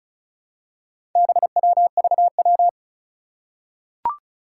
Morse code, heard as a keyed tone, about 36 wpm, 700 Hz. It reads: BWVW
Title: Four Challenging Letter-Number Combinations 35wpm